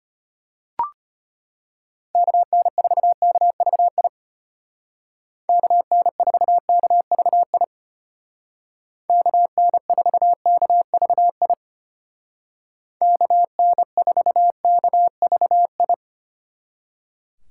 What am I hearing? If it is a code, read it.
Morse code, heard as a keyed tone, about 33 wpm, 700 Hz. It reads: KN4KVI KN4KVI KN4KVI KN4KVI